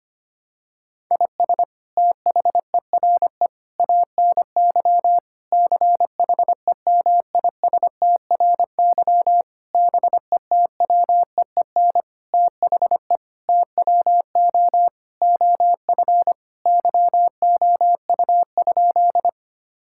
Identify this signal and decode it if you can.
Morse code, heard as a keyed tone, about 25 wpm, 700 Hz. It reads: IS THERE ANY CHEMISTRY BETWEEN THE TWO OF YOU?
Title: Sentences from Top 1000 Words - Review Part 6 - 25wpm